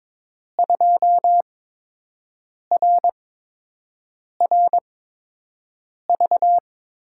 Morse code, heard as a keyed tone, about 21 wpm, spaced as for 6 wpm, 700 Hz. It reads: 2RRV
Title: USA Call Signs - 6 Characters - 8x Character Spacing - 22wpm